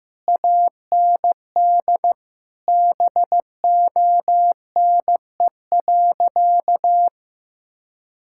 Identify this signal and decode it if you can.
Morse code, heard as a keyed tone, about 16 wpm, 700 Hz. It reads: AND BONE.
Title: Sentences from Top 1600 Words 15wpm